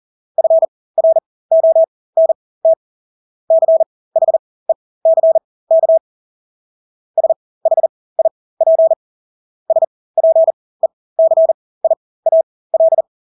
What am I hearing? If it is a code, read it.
Morse code, heard as a keyed tone, about 40 wpm, spaced as for 20 wpm, 650 Hz. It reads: FRONT CHECK SHIP SPECIAL
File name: Engelske ord 40-20 wpm